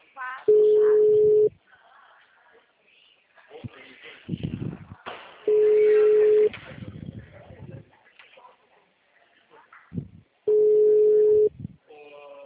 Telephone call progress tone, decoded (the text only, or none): ringback tone